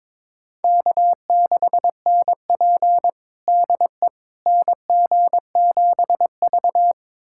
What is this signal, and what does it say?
Morse code, heard as a keyed tone, about 22 wpm, 700 Hz. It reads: K6NP DE NG7V
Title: USA Callsign DE Callsign - 4 Characters - 22wpm